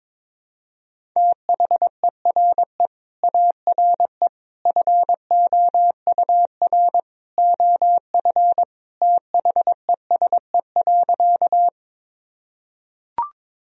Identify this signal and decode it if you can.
Morse code, heard as a keyed tone, about 22 wpm, 700 Hz. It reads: THERE ARE FOUR OF THESE.
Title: Sentences from Top 1000 Words - Review Part 1 - 22wpm